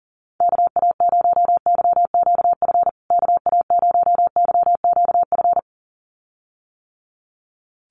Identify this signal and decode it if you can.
Morse code, heard as a keyed tone, about 40 wpm, 700 Hz. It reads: KA0YQF KA0YQF
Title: USA Call Signs Repeated 2x 40wpm